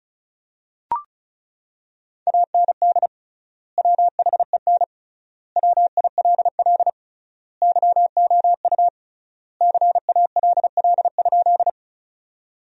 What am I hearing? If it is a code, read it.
Morse code, heard as a keyed tone, about 36 wpm, 700 Hz. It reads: AND WHEN WILL YOU CALL?
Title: 3x Word-Spacing - Sentences from Top 100 Words - 35wpm